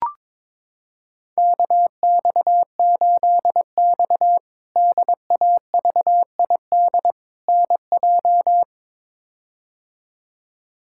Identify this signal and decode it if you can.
Morse code, heard as a keyed tone, about 22 wpm, 700 Hz. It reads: KX8X DAVID NJ
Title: K1USN SST QSO Practice - 22wpm